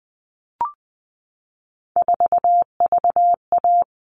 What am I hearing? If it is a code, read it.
Morse code, heard as a keyed tone, about 20 wpm, 700 Hz. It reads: 4VA
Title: All Cut Numbers - 3-Digit 20wpm